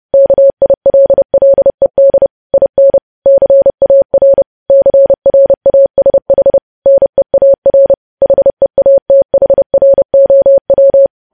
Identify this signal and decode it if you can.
Morse code, heard as a keyed tone, about 30 wpm, 550 Hz. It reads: KILLED IN CAR CRASH NEAR HEATHROW
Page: news headlines in morse code at 30 wpm